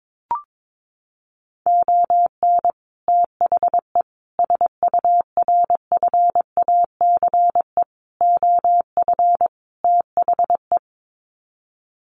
Morse code, heard as a keyed tone, about 22 wpm, 700 Hz. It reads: ON THE SURFACE OF THE
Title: Sets of 5 Words Encore 22wpm